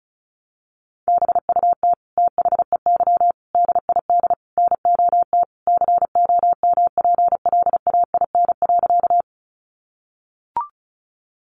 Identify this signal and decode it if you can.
Morse code, heard as a keyed tone, about 34 wpm, 700 Hz. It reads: BUT THEY DID NOT COMPLAIN.
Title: Sentences from Top 1300 Words 35wpm